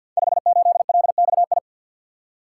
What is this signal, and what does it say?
Morse code, heard as a keyed tone, about 47 wpm, 700 Hz. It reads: 58LXI
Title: Code Groups - 5 Letter:Numbers - 50wpm